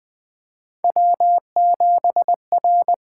Morse code, heard as a keyed tone, about 20 wpm, 700 Hz. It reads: W7R
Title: USA Call Signs - 8x Character Spacing - 20wpm